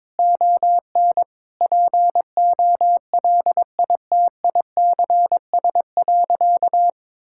Morse code, heard as a keyed tone, about 22 wpm, 700 Hz. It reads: ON POLITICS.
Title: Sentences from Top 1000 Words - Review Part 6 - 22wpm